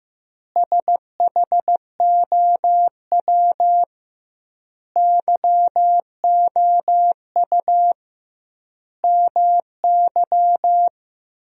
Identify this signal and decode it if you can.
Morse code, heard as a keyed tone, about 15 wpm, 700 Hz. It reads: SHOW YOU MY
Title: 2x Word-Spacing - Sentences from Top 500 Words - 15wpm